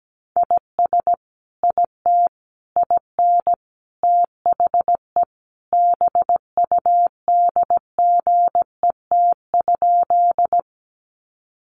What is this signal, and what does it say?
Morse code, heard as a keyed tone, about 17 wpm, 700 Hz. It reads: IS IT IN THE BUDGET?
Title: Sentences from Top 1200 Words 17wpm